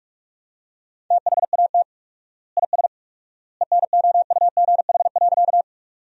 Morse code, heard as a keyed tone, about 45 wpm, 700 Hz. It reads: THAT IS ENOUGH.
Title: 4x Word-Spacing - Sentences from Top 400 Words - 45wpm